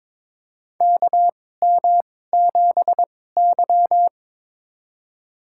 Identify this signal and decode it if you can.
Morse code, heard as a keyed tone, about 22 wpm, spaced as for 16 wpm, 700 Hz. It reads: KM7Y
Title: USA Call Signs - 4 Characters - 2x Character Spacing - 22wpm